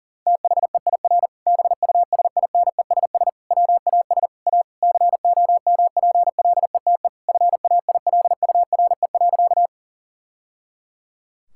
Morse code, heard as a keyed tone, about 40 wpm, 700 Hz. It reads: THEIR BUSINESS WAS A COMPLETE FAILURE.
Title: Sentences from Top 1000 Words - Review Part 6 - 40wpm